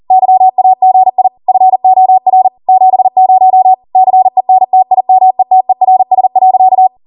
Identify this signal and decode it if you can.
Morse code, heard as a keyed tone, about 40 wpm, 750 Hz. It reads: YAGI FOR 70 CENTIMETERS.